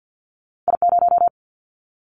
Morse code, heard as a keyed tone, about 50 wpm, 700 Hz. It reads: I0
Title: Callsign Prefixes for Top 25 DXCC - Repeated 3x - 50wpm